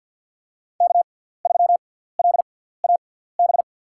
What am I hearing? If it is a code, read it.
Morse code, heard as a keyed tone, about 49 wpm, spaced as for 17 wpm, 690 Hz. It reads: K3LAB
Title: USA Call Signs - 6x Character Spacing - 50wpm